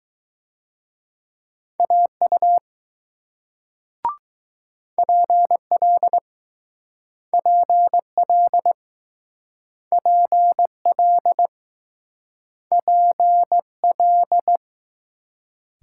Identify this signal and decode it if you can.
Morse code, heard as a keyed tone, about 21 wpm, 700 Hz. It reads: AU PL PL PL PL